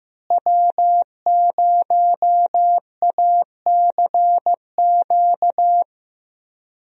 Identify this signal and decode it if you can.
Morse code, heard as a keyed tone, about 16 wpm, 700 Hz. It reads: W0ACQ
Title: USA Call Signs - 5 Characters - Repeated 3x - 15wpm